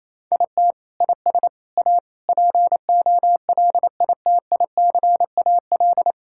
Morse code, heard as a keyed tone, about 27 wpm, 700 Hz. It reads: IT IS A POLITICAL
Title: Sentences from Top 1000 Words - Review Part 5 - 28wpm